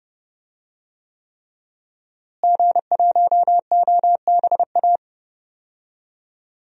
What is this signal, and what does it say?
Morse code, heard as a keyed tone, about 30 wpm, 700 Hz. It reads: G1OBA
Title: DXCC-100 Call Signs - 5 Characters - Repeated 3x - 30wpm